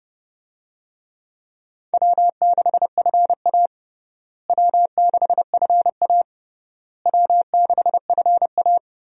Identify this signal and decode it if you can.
Morse code, heard as a keyed tone, about 32 wpm, 700 Hz. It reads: W6FA W6FA W6FA
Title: USA Call Signs - 4 Characters - Repeated 3x - 30wpm